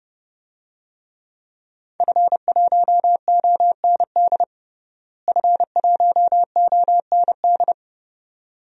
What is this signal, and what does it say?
Morse code, heard as a keyed tone, about 30 wpm, 700 Hz. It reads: F1OND F1OND
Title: DXCC-100 Call Signs - 5 Characters - Repeated 2x - 30wpm